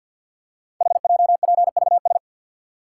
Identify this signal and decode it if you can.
Morse code, heard as a keyed tone, about 50 wpm, 690 Hz. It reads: HJPVS
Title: Code Groups - 5 Letters - 50wpm